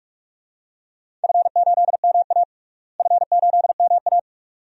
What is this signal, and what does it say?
Morse code, heard as a keyed tone, about 46 wpm, 700 Hz. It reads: F8MA F8MA
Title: DXCC-100 Call Signs - 4 Characters - Repeated 2x - 45wpm